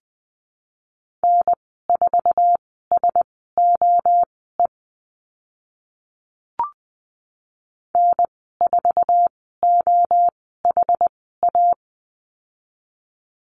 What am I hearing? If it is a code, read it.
Morse code, heard as a keyed tone, about 20 wpm, spaced as for 14 wpm, 700 Hz. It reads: N4SOE N4OHA
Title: 082 Lesson 14g - Number 2 - Call Signs 20wpm